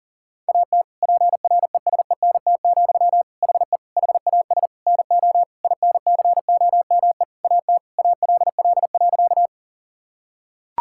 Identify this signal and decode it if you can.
Morse code, heard as a keyed tone, about 40 wpm, 700 Hz. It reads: AT PRESENT, HE HAS NO INCOME AT ALL.
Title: Sentences from Top 1000 Words - Review Part 5 - 40wpm